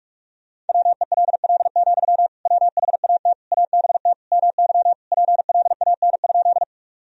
Morse code, heard as a keyed tone, about 41 wpm, 700 Hz. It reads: WELL, WHAT ABT MY PLAN?
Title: Sentences from Top 500 Words and Abbreviations 45wpm